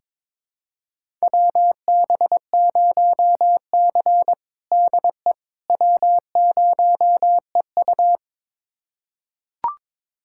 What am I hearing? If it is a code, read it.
Morse code, heard as a keyed tone, about 22 wpm, 700 Hz. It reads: WB0C DE W0EU
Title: USA Callsign DE Callsign - 4 Characters - 22wpm